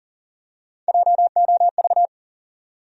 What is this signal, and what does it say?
Morse code, heard as a keyed tone, about 40 wpm, 700 Hz. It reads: JOV
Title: USA Call Sign Postfixes - 3 Letters 40wpm